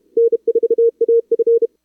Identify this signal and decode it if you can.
Morse code, heard as a keyed tone, about 34 wpm, 440 Hz. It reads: N4AF
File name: CWSprint / Feb2022 / k5tr-feb-na-sprint-cw-k5ot -0000